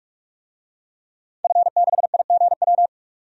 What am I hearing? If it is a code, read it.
Morse code, heard as a keyed tone, about 46 wpm, 700 Hz. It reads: F6IGW
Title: DXCC-100 Call Signs - 5 Characters - 45wpm